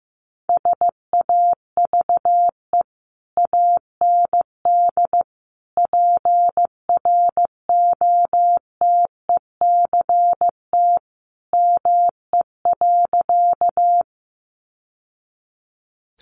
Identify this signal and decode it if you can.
Morse code, heard as a keyed tone, about 15 wpm, 700 Hz. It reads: SAVE AND PROTECT ME.
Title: Sentences from Top 1100 Words 15wpm